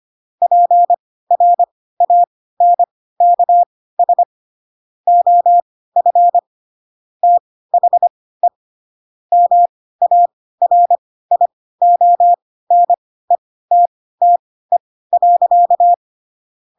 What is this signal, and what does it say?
Morse code, heard as a keyed tone, about 25 wpm, spaced as for 16 wpm, 700 Hz. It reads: PRANKS OF THE MARIONETTE.
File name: Pinocchio / Pinocchio ch3 10 25 ik1hge